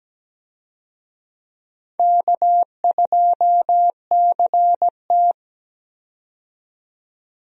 Morse code, heard as a keyed tone, about 17 wpm, 700 Hz. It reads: K2CT
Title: USA Call Signs - 4 Characters - 17wpm